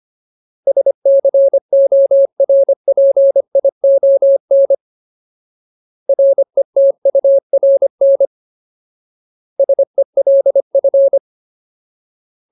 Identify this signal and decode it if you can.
Morse code, heard as a keyed tone, about 25 wpm, 550 Hz. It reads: SCORPION RETURN SELF
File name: Ultimate Word List / 25WPM Ultimate